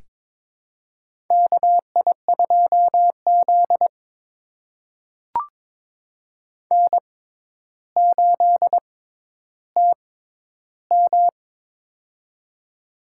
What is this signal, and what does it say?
Morse code, heard as a keyed tone, about 22 wpm, spaced as for 11 wpm, 700 Hz. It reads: KI2Z N8TM
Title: USA Call Signs - 4 Characters - 6x Character Spacing - 22wpm